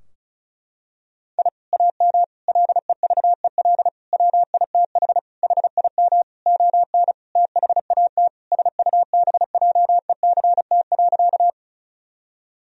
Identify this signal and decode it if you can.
Morse code, heard as a keyed tone, about 36 wpm, 700 Hz. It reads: I AM LEVEL WITH HIM ON THAT SUBJECT.
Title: Sentences from Top 1000 Words - Review Part 4 - 35wpm